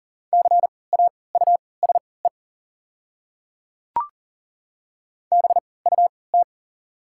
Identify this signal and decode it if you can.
Morse code, heard as a keyed tone, about 40 wpm, spaced as for 23 wpm, 690 Hz. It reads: CAUSE BUT
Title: 3x Character Spacing - Top 200 Words - 40wpm